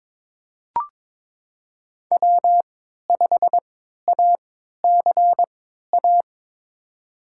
Morse code, heard as a keyed tone, about 22 wpm, spaced as for 13 wpm, 700 Hz. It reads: W5ACA
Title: USA Call Signs - 5 Characters - 3x Character Spacing - 22wpm